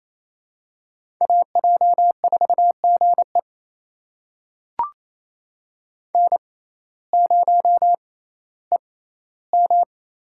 Morse code, heard as a keyed tone, about 28 wpm, spaced as for 15 wpm, 700 Hz. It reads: AJ4GE N0EM
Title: USA Call Signs - 5 Characters - 6x Character Spacing - 28wpm